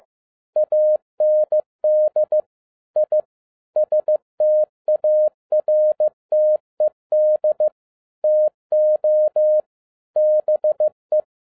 Morse code, heard as a keyed tone, about 15 wpm, 610 Hz. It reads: AND I STARTED TO BE